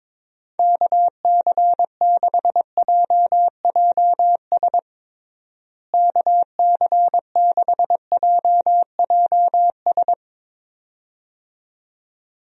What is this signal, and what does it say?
Morse code, heard as a keyed tone, about 23 wpm, 700 Hz. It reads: KC6JJS KC6JJS
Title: USA Call Signs - 6 Characters - Repeated 2x - 22wpm